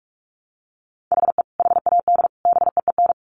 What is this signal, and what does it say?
Morse code, heard as a keyed tone, about 44 wpm, 700 Hz. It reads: HE HAD BEEN
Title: Sets of 3 Words 45wpm